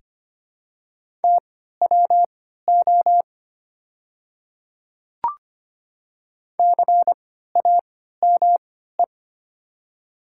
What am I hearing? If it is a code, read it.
Morse code, heard as a keyed tone, about 25 wpm, spaced as for 14 wpm, 700 Hz. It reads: TWO CAME
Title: 3x Character Spacing - Top 200 Words - 25wpm